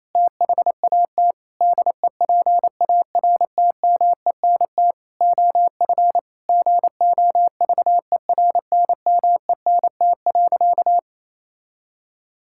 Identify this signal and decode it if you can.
Morse code, heard as a keyed tone, about 28 wpm, 700 Hz. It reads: THAT DEPARTMENT OF GOVERNMENT.
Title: Sentences from Top 1000 Words - Review Part 5 - 28wpm